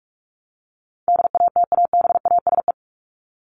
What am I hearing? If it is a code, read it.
Morse code, heard as a keyed tone, about 44 wpm, 700 Hz. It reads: DATABASE
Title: Top 1751-2000 Words 45wpm